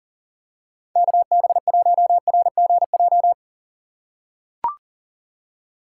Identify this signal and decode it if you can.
Morse code, heard as a keyed tone, about 39 wpm, 700 Hz. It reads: KB1RGJ
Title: USA Call Signs - 6 Characters - 4x Character Spacing - 40wpm